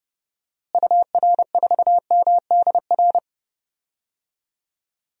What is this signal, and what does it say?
Morse code, heard as a keyed tone, about 30 wpm, 700 Hz. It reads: UR4MDR
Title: DXCC-100 Call Signs - 6 Characters - Repeated 3x - 30wpm